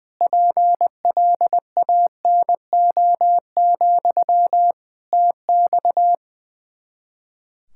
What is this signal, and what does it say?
Morse code, heard as a keyed tone, about 21 wpm, 700 Hz. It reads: PLANO, TX
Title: Practice QSO - Exchange 1 - Beginning - 20wpm